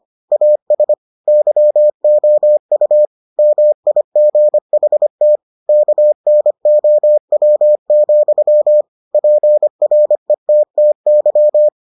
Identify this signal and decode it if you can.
Morse code, heard as a keyed tone, about 25 wpm, 600 Hz. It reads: AS YOU MIGHT KNOW, PRETTY